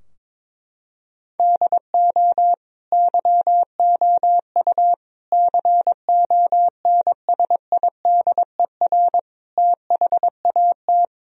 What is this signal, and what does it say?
Morse code, heard as a keyed tone, about 22 wpm, 700 Hz. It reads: DO YOU CONSIDER THAT
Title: Sentences from Top 1000 Words - Review Part 4 - 22wpm